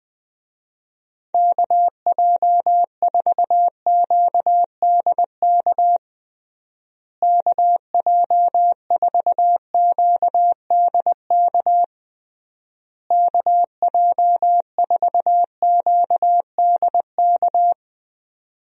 Morse code, heard as a keyed tone, about 20 wpm, 700 Hz. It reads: KJ4QDK KJ4QDK KJ4QDK
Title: USA Call Signs - 6 Characters - Repeated 3x - 20wpm